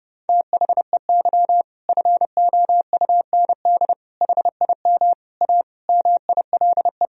Morse code, heard as a keyed tone, about 30 wpm, 700 Hz. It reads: THEY FOUND HIM A MILE
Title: Sentences from Top 1000 Words - Review Part 1 - 30wpm